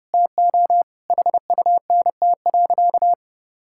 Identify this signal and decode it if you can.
Morse code, heard as a keyed tone, about 30 wpm, 700 Hz. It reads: TO HUNT.
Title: Sentences from Top 1600 Words 30wpm